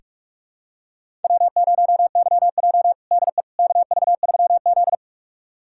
Morse code, heard as a keyed tone, about 46 wpm, 700 Hz. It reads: W0YJ DE KV3Z
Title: USA Callsign DE Callsign - 4 Characters - 45wpm